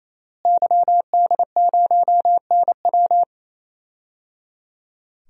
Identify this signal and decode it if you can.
Morse code, heard as a keyed tone, about 28 wpm, 690 Hz. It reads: YD0NW
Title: DXCC-100 Call Signs - 5 Characters - Repeated 3x - 28wpm